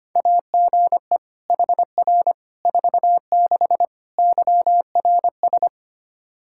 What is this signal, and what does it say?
Morse code, heard as a keyed tone, about 25 wpm, 700 Hz. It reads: AGE HR 46 YRS
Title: Practice QSO - Exchange 3 - More Info - 25wpm